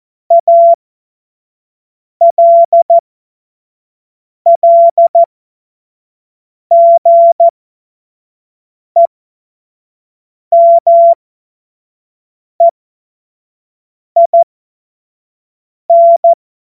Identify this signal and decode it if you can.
Morse code, heard as a keyed tone, about 14 wpm, spaced as for 5 wpm, 680 Hz. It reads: ALLGEMEIN
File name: Morsekurs / Witze / Witz 07. Dezember 2021, 25 F70